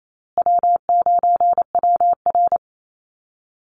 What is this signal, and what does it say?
Morse code, heard as a keyed tone, about 28 wpm, 700 Hz. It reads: W9WR
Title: Easy Call Signs Repeated 3x - 28wpm